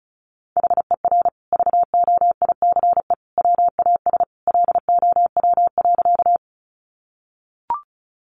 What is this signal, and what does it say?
Morse code, heard as a keyed tone, about 34 wpm, 700 Hz. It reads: HER VOICE WAS LOW.
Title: Sentences from Top 500 Words 35wpm